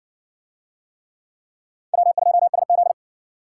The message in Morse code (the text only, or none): W2SZ